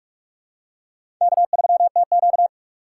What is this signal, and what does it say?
Morse code, heard as a keyed tone, about 46 wpm, 700 Hz. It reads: K3TQ
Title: USA Call Signs - 4 Characters - 2x Character Spacing - 45wpm